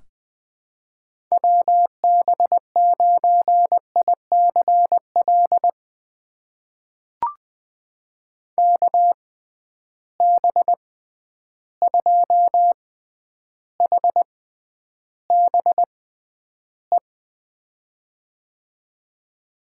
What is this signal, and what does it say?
Morse code, heard as a keyed tone, about 21 wpm, spaced as for 10 wpm, 700 Hz. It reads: WB9ICL KB2HBE
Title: USA Call Signs - 6 Characters - 6x Character Spacing - 20wpm